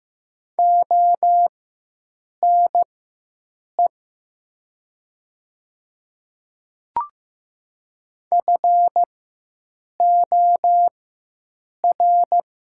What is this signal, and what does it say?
Morse code, heard as a keyed tone, about 15 wpm, spaced as for 7 wpm, 700 Hz. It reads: ONE FOR
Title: 4x Character Spacing - Top 100 Words - 15wpm